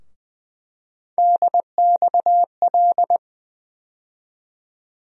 Morse code, heard as a keyed tone, about 20 wpm, 700 Hz. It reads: DXL
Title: USA Call Sign Postfixes - 3 Letters 20wpm